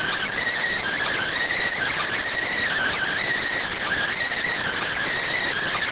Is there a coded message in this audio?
Yes: Emergency Alert System (SAME) tones (header bursts) from 0.8 to 5.1 s, heard as harsh buzzing bursts.